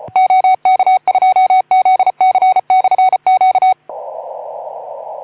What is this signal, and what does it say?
Morse code, heard as a keyed tone, about 28 wpm, 750 Hz. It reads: OK2ZC/Q
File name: Archiv / CUC181 / OK2ZC CUC 181